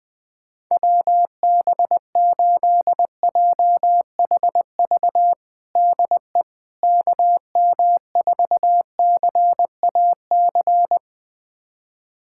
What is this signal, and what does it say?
Morse code, heard as a keyed tone, about 20 wpm, 700 Hz. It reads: WB8JHV DE KM4CAC
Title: USA Callsign DE Callsign 20wpm